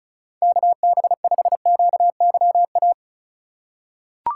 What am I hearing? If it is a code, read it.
Morse code, heard as a keyed tone, about 36 wpm, 700 Hz. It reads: KB5QYA
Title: USA Call Signs - 6 Characters - 2x Character Spacing - 35wpm